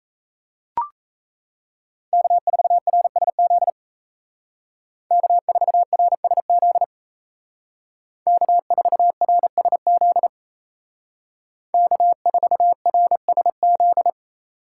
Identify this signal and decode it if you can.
Morse code, heard as a keyed tone, about 34 wpm, 700 Hz. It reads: K4RSZ K4RSZ K4RSZ K4RSZ
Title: Speed Racing - USA 1x3 Call Signs 28wpm